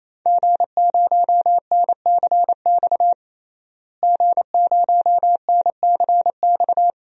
Morse code, heard as a keyed tone, about 28 wpm, 700 Hz. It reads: G0NCX G0NCX
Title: DXCC-100 Call Signs - 5 Characters - Repeated 3x - 28wpm